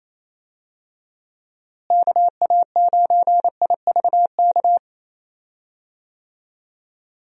Morse code, heard as a keyed tone, about 28 wpm, 700 Hz. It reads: KA9IVK
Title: USA Call Signs - 6 Characters - Repeated 2x - 28wpm